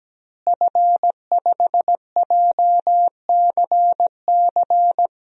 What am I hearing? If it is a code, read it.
Morse code, heard as a keyed tone, about 17 wpm, 700 Hz. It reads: F5JCC
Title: DXCC-100 Call Signs - 5 Characters - Repeated 3x - 17wpm